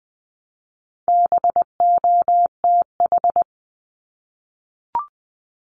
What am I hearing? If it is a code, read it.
Morse code, heard as a keyed tone, about 20 wpm, 700 Hz. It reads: BOTH
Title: Top 500 Words Repeated 20wpm